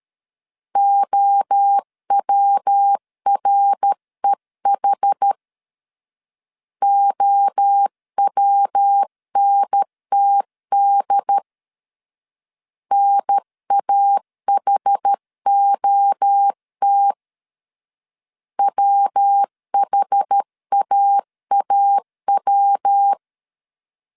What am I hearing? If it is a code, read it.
Morse code, heard as a keyed tone, about 13 wpm, 800 Hz. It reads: OWREH OWNTD NAHOT WHAAW
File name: CW / DET / 1 3 1500C S A 12wpm 2905